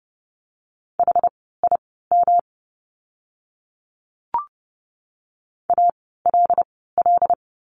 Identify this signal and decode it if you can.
Morse code, heard as a keyed tone, about 30 wpm, spaced as for 17 wpm, 700 Hz. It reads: HIM ALL
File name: course / 30:10wpm / 237 Lesson 38f - Letter q - Words 30wpm